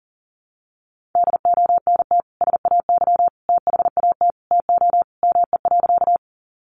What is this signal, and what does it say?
Morse code, heard as a keyed tone, about 40 wpm, 700 Hz. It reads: DONT SAY THAT TO ME.
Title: Sentences from Top 300 Words 40wpm